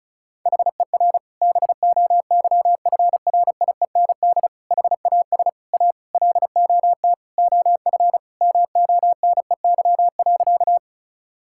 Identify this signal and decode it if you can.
Morse code, heard as a keyed tone, about 34 wpm, 700 Hz. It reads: HER BOYFRIEND HAS A LOT OF MONEY.
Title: Sentences from Top 1000 Words - Review Part 5 - 35wpm